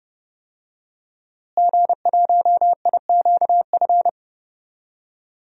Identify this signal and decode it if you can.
Morse code, heard as a keyed tone, about 32 wpm, 700 Hz. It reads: G1IQF
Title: DXCC-100 Call Signs - 5 Characters - Repeated 2x - 30wpm